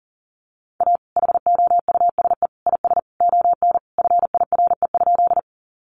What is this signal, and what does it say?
Morse code, heard as a keyed tone, about 40 wpm, 700 Hz. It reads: A HOUSE IS ON FIRE?
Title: Sentences from Top 400 Words 40wpm